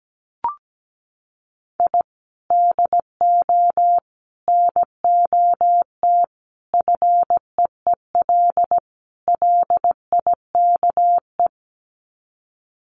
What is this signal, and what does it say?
Morse code, heard as a keyed tone, about 17 wpm, 700 Hz. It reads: I DO NOT FEEL LIKE
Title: Sets of 5 Words Encore 17wpm